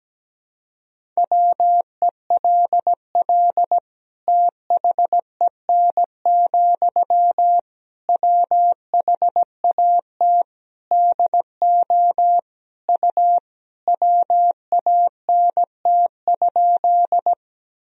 Morse code, heard as a keyed tone, about 17 wpm, 700 Hz. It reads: WELL THEN, WHAT DO U WANT?
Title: Sentences from Top 200 Words and Abbreviations 17wpm